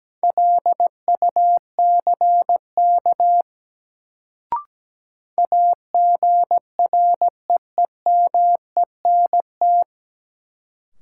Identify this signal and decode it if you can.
Morse code, heard as a keyed tone, about 17 wpm, 700 Hz. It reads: LUCK AGREEMENT
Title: Sentences from Top 1500 Words - New Word Drill - 17wpm